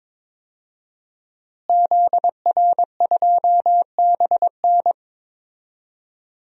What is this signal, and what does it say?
Morse code, heard as a keyed tone, about 22 wpm, 700 Hz. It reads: ZR2BN
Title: DXCC-100 Call Signs - 5 Characters - Repeated 3x - 22wpm